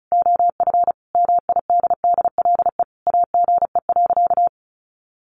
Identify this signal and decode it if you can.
Morse code, heard as a keyed tone, about 34 wpm, 700 Hz. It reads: OF MIDDLE AGE.
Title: Sentences from Top 1400 Words 35wpm